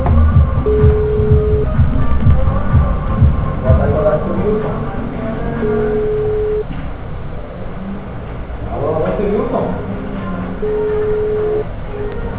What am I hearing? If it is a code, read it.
Ringback tone (Europe, 425 Hz), with speech.